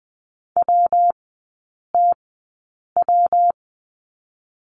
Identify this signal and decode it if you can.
Morse code, heard as a keyed tone, about 20 wpm, 700 Hz. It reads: W T W
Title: 114 Lesson 19e - Letter w - Three Characters 20wpm